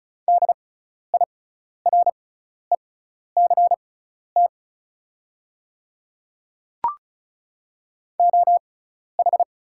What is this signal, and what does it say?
Morse code, heard as a keyed tone, about 36 wpm, spaced as for 12 wpm, 700 Hz. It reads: DIRECT OH
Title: 6x Character Spacing - Top 500 Words - 35wpm